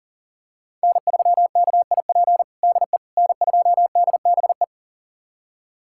Morse code, heard as a keyed tone, about 40 wpm, 700 Hz. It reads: N3KIP DE N2DBE
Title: USA Callsign DE Callsign - 5 Characters - 40wpm